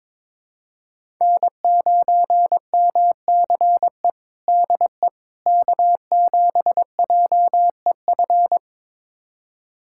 Morse code, heard as a keyed tone, about 21 wpm, 700 Hz. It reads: N9MCE DE K7JEF
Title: USA Callsign DE Callsign - 5 Characters - 22wpm